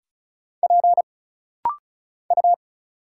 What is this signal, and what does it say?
Morse code, heard as a keyed tone, about 36 wpm, 700 Hz. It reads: P U
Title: Single Letter-Number - Rapid-Fire 35wpm